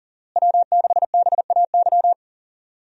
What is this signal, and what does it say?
Morse code, heard as a keyed tone, about 37 wpm, 700 Hz. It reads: W6BAY
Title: USA Call Signs - 5 Characters - Repeated 2x - 40wpm